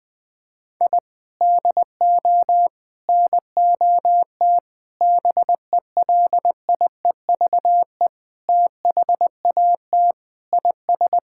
Morse code, heard as a keyed tone, about 20 wpm, 700 Hz. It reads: I DO NOT BELIEVE THAT IS
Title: Sentences with 4-Word Phrases - 20wpm